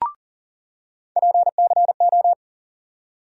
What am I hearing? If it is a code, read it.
Morse code, heard as a keyed tone, about 40 wpm, 700 Hz. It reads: PCO
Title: USA Call Sign Postfixes - 3 Letters 40wpm